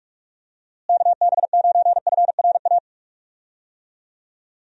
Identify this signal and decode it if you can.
Morse code, heard as a keyed tone, about 44 wpm, 700 Hz. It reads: KB9FRA
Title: USA Call Signs - 6 Characters - Repeated 3x - 45wpm